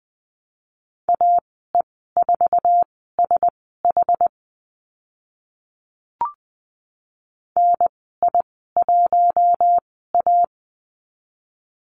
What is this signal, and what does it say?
Morse code, heard as a keyed tone, about 20 wpm, spaced as for 14 wpm, 700 Hz. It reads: AE4SH NI1A
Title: 069 Lesson 12g - Letter d - Call Signs 20wpm